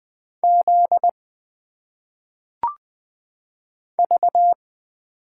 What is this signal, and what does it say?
Morse code, heard as a keyed tone, about 20 wpm, 700 Hz. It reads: Z V
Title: Single Letters - Phonetic Pronunciation 20wpm